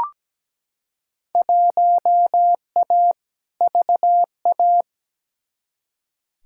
Morse code, heard as a keyed tone, about 17 wpm, 700 Hz. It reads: 1A VA
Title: Field Day Exchanges Encore - 17wpm